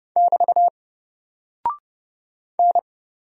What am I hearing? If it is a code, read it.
Morse code, heard as a keyed tone, about 30 wpm, 700 Hz. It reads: <BT> N
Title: Single Character (Letter:Number:Punctuation:Prosign) 30wpm